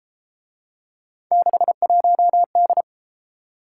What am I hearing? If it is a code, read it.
Morse code, heard as a keyed tone, about 32 wpm, 700 Hz. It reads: 61D